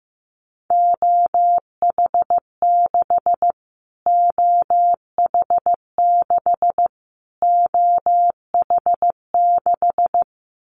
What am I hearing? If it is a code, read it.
Morse code, heard as a keyed tone, about 15 wpm, 700 Hz. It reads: OH6 OH6 OH6
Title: Callsign Prefixes for Top 25 DXCC - Repeated 3x - 15wpm